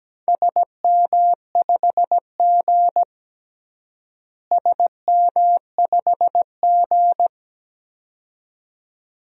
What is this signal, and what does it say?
Morse code, heard as a keyed tone, about 17 wpm, 700 Hz. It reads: SM5G SM5G
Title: DXCC-100 Call Signs - 4 Characters - Repeated 2x - 17wpm